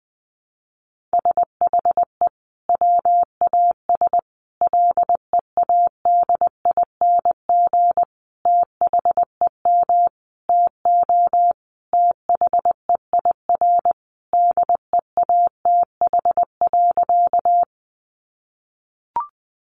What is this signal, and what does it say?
Morse code, heard as a keyed tone, about 20 wpm, 700 Hz. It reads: SHE WAS LEADING THEM TO THEIR DEATH.
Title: Sentences from Top 1100 Words 20wpm